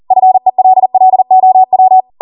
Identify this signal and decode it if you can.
Morse code, heard as a keyed tone, about 40 wpm, 750 Hz. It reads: FELLOW